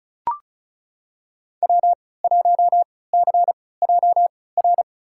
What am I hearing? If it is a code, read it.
Morse code, heard as a keyed tone, about 36 wpm, spaced as for 20 wpm, 700 Hz. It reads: W1CJR
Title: USA Call Signs - 5 Characters - 3x Character Spacing - 35wpm